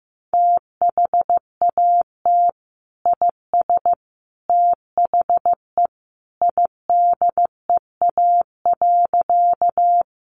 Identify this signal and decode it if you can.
Morse code, heard as a keyed tone, about 15 wpm, 700 Hz. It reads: THAT IS THE IDEA.
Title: Sentences from Top 300 Words 15wpm